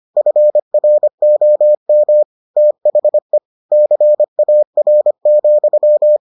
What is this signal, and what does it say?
Morse code, heard as a keyed tone, about 25 wpm, 600 Hz. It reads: FROM THE CAR,